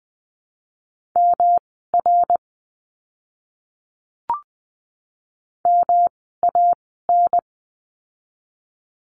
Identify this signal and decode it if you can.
Morse code, heard as a keyed tone, about 20 wpm, spaced as for 14 wpm, 700 Hz. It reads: MR MAN
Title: 244 Lesson 39f - Letter z - Words 20wpm